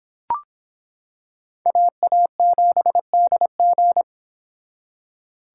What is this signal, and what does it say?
Morse code, heard as a keyed tone, about 26 wpm, 700 Hz. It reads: AA7DG